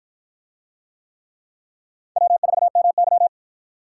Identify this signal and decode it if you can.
Morse code, heard as a keyed tone, about 52 wpm, 700 Hz. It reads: W4MY